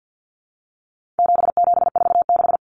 Morse code, heard as a keyed tone, about 50 wpm, 700 Hz. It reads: 7746
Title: Numbers - 4-Digit 50wpm